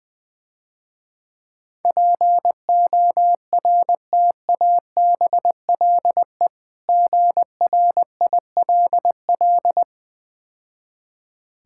Morse code, heard as a keyed tone, about 21 wpm, 700 Hz. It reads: PORTABLE GRILL
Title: Difficult Words - Speed-Racing with Hints - 20wpm